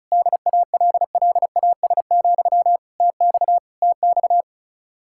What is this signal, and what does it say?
Morse code, heard as a keyed tone, about 45 wpm, 700 Hz. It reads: DALLAS, TX TX